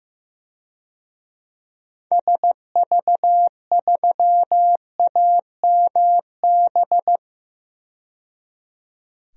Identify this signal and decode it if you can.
Morse code, heard as a keyed tone, about 15 wpm, 700 Hz. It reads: SV3AMB
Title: DXCC-100 Call Signs - 6 Characters - 15wpm